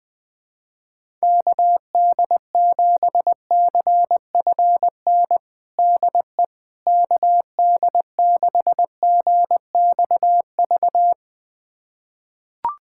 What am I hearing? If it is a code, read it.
Morse code, heard as a keyed tone, about 20 wpm, 700 Hz. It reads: KD7CFN DE KD6GXV
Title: USA Callsign DE Callsign 20wpm